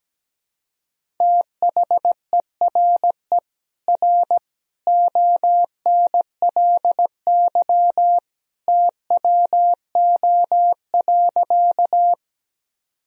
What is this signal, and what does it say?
Morse code, heard as a keyed tone, about 17 wpm, 700 Hz. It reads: THERE R ONLY TWO.
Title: Sentences from Top 200 Words and Abbreviations 17wpm